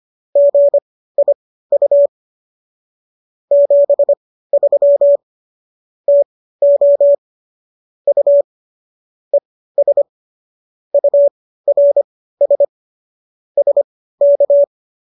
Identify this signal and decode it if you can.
Morse code, heard as a keyed tone, about 25 wpm, spaced as for 15 wpm, 570 Hz. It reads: GIU 73 TO U ES URS SK